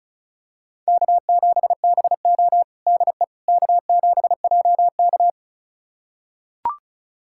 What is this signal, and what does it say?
Morse code, heard as a keyed tone, about 36 wpm, 700 Hz. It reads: K7BO DE K7JK
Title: USA Callsign DE Callsign - 4 Characters - 35wpm